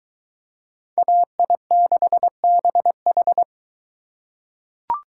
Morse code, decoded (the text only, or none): AI6BH